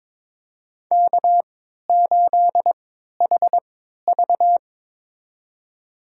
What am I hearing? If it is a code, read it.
Morse code, heard as a keyed tone, about 22 wpm, spaced as for 13 wpm, 700 Hz. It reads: K8HV
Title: USA Call Signs - 4 Characters - 3x Character Spacing - 22wpm